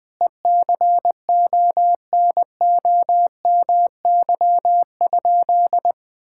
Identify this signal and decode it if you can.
Morse code, heard as a keyed tone, about 20 wpm, 700 Hz. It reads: ECONOMY?
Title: Sentences from Top 1000 Words - Review Part 5 - 20wpm